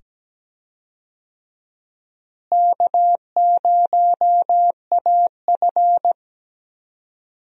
Morse code, heard as a keyed tone, about 17 wpm, 700 Hz. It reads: K0AF
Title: USA Call Signs - 4 Characters - Repeated 2x - 17wpm